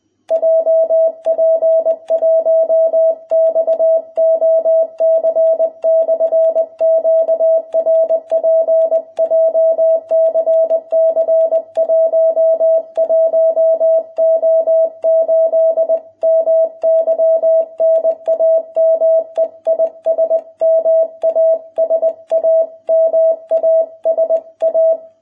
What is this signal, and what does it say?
Morse code, heard as a keyed tone, about 20 wpm, 660 Hz. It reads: JP1XOC/QRP JCC 11O8 MY NAME IS MASA MASA